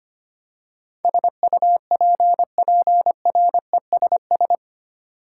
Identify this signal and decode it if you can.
Morse code, heard as a keyed tone, about 25 wpm, 700 Hz. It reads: SUPPRESS
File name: 8 Letter Words - First 3 Letters Given - 25wpm